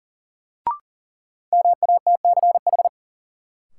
Morse code, heard as a keyed tone, about 40 wpm, 700 Hz. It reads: MATCH
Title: Sentences from Top 1600 Words - New Word Drill Alt - 40wpm